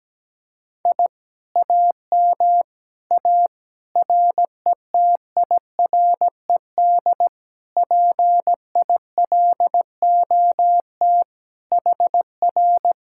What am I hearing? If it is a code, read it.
Morse code, heard as a keyed tone, about 17 wpm, 700 Hz. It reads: I AM A RETIRED PILOT HR
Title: Speed-Racing Common QSO Phrases - 17wpm